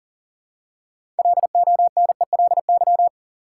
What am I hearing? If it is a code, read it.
Morse code, heard as a keyed tone, about 40 wpm, 700 Hz. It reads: LONELY
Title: Six Letter Words - 40wpm